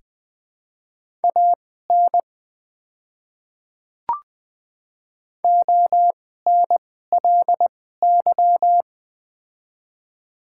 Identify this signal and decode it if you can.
Morse code, heard as a keyed tone, about 20 wpm, spaced as for 14 wpm, 700 Hz. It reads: AN ONLY
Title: 2x Character Spacing - Top 500 Words - 20wpm